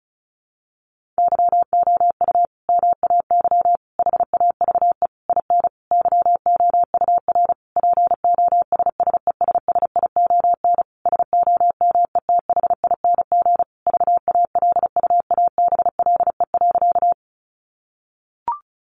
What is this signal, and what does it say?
Morse code, heard as a keyed tone, about 34 wpm, 700 Hz. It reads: YOU MAY HAVE IN YOUR POSSESSION SOMETHING VALUABLE.
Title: Sentences from Top 1400 Words 35wpm